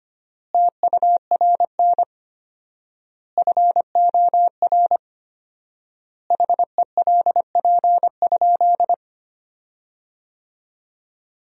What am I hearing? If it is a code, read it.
Morse code, heard as a keyed tone, about 25 wpm, 700 Hz. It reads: TURN FOR HELP?
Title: 4x Word-Spacing - Sentences from Top 200 Words - 25wpm